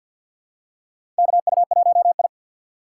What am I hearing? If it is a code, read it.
Morse code, heard as a keyed tone, about 53 wpm, 700 Hz. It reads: KU1I
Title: USA Call Signs - 6x Character Spacing - 50wpm